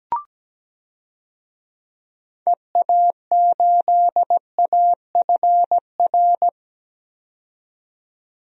Morse code, heard as a keyed tone, about 17 wpm, 700 Hz. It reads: EA8AFR
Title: DXCC-100 Call Signs - 6 Characters - 17wpm